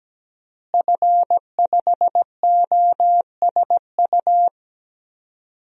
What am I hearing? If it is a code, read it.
Morse code, heard as a keyed tone, about 17 wpm, 700 Hz. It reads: F5OSU
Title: DXCC-100 Call Signs - 5 Characters - Repeated 2x - 17wpm